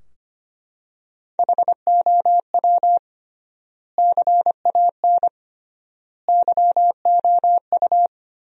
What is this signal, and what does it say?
Morse code, heard as a keyed tone, about 25 wpm, 700 Hz. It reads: HOW CAN YOU
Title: 3x Word-Spacing - Sentences from Top 100 Words - 25wpm